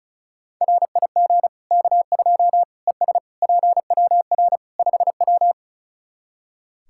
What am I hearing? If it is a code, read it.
Morse code, heard as a keyed tone, about 36 wpm, 700 Hz. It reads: RIG K2 ES PWR 5W
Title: Practice QSO - Exchange 2 - Info - 35wpm